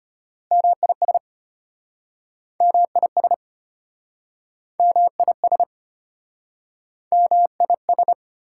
Morse code, heard as a keyed tone, about 33 wpm, 700 Hz. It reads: MIS MIS MIS MIS